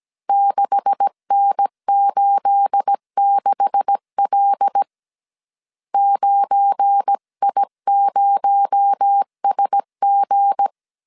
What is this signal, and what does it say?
Morse code, heard as a keyed tone, about 17 wpm, 800 Hz. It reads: 6N86L 9I0SG
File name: CW / DET / 1 9 1500C S A 16wpm 2828